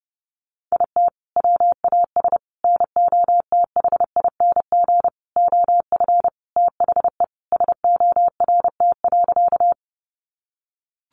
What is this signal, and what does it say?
Morse code, heard as a keyed tone, about 28 wpm, 700 Hz. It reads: IT WAS NOTHING OF THE SORT.
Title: Sentences from Top 1300 Words 30wpm